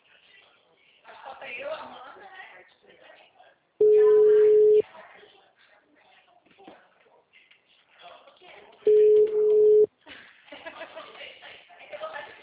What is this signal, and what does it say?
Ringback tone (Europe, 425 Hz), with speech.